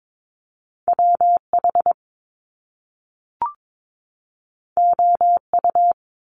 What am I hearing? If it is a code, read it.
Morse code, heard as a keyed tone, about 22 wpm, 700 Hz. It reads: W H O U
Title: Top Two Letter Combinations in Words 22wpm